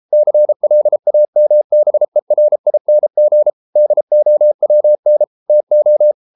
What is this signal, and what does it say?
Morse code, heard as a keyed tone, about 33 wpm, 600 Hz. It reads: CLAMBERING DOWN TO